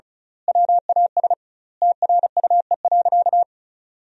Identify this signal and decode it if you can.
Morse code, heard as a keyed tone, about 34 wpm, 700 Hz. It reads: WAS TRUE.
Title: 2x Word-Spacing - Sentences from Top 400 Words - 35wpm